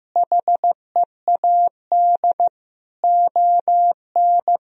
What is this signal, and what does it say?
Morse code, heard as a keyed tone, about 15 wpm, 700 Hz. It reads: HEAD ON
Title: Sentences from Top 1000 Words - Review Part 4 - 15wpm